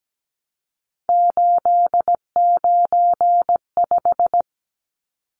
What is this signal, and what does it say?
Morse code, heard as a keyed tone, about 17 wpm, 700 Hz. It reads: 895
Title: Numbers - 3-Digit 17wpm